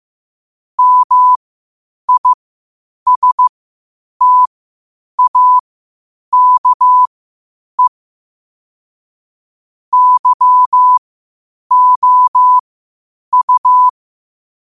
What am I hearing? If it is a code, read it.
Morse code, heard as a keyed tone, about 15 wpm, spaced as for 8 wpm, 1000 Hz. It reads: MISTAKE YOU
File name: qotdfiles / 8wpm